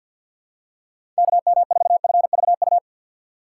Morse code, heard as a keyed tone, about 46 wpm, 690 Hz. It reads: KM4FVU
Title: USA Call Signs - 6x Character Spacing - 50wpm